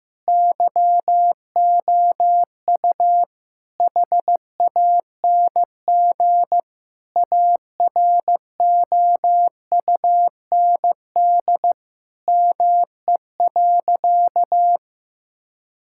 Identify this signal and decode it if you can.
Morse code, heard as a keyed tone, about 15 wpm, 700 Hz. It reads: YOU HANG AROUND ME.
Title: Sentences from Top 1600 Words 15wpm